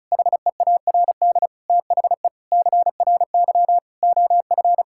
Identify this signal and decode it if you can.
Morse code, heard as a keyed tone, about 34 wpm, 700 Hz. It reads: HEARD THE CRY OF
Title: Sentences from Top 1000 Words - Review Part 5 - 35wpm